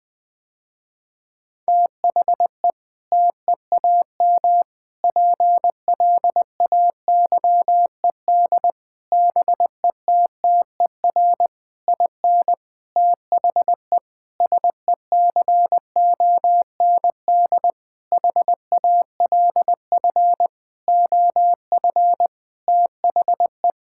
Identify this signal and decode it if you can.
Morse code, heard as a keyed tone, about 21 wpm, 700 Hz. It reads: THE TEAM PLAYED BETTER IN THE SECOND HALF OF THE
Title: Sentences with 6-Word Phrases - Spoken Beginning - 20wpm